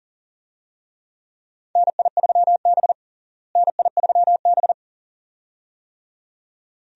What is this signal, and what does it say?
Morse code, heard as a keyed tone, about 40 wpm, 700 Hz. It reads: NI3B NI3B
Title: USA Call Signs - 4 Characters - Repeated 2x - 40wpm